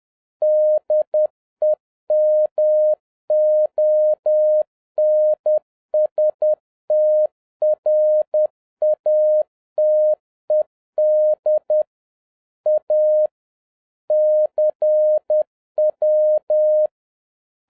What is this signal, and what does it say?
Morse code, heard as a keyed tone, about 10 wpm, 610 Hz. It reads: DEMONSTRATED A CW